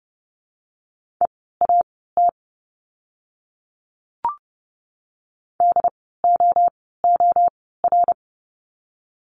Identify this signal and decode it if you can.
Morse code, heard as a keyed tone, about 30 wpm, spaced as for 17 wpm, 700 Hz. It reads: EAT DOOR
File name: Lesson 17f - Letter u - Words 30wpm